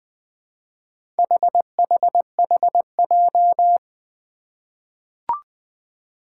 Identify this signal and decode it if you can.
Morse code, heard as a keyed tone, about 20 wpm, 700 Hz. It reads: HHHJ
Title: Four Challenging Letter-Number Combinations 20wpm